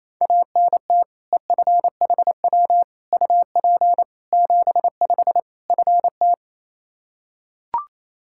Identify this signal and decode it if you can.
Morse code, heard as a keyed tone, about 27 wpm, 700 Hz. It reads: ANT EFHW UP 75 FT
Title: K1USN SST QSO Practice 28wpm